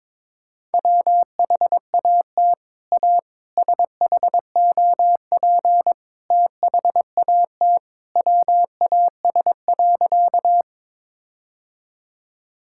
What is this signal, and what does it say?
Morse code, heard as a keyed tone, about 22 wpm, 700 Hz. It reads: WHAT A SHOP THAT WAS.
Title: Sentences from Top 1600 Words 22wpm